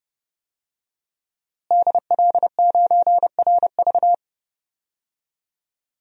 Morse code, heard as a keyed tone, about 30 wpm, 700 Hz. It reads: DL9RV
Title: DXCC-100 Call Signs - 5 Characters - Repeated 2x - 30wpm